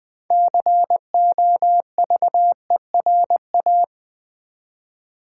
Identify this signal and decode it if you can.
Morse code, heard as a keyed tone, about 20 wpm, 700 Hz. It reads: COVERA
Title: Build A Word - 2 Letters At A Time - 8-Letter Words - 20wpm